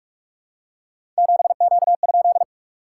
Morse code, heard as a keyed tone, about 44 wpm, 700 Hz. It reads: 7Q?
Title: SOTA QSO Practice 45wpm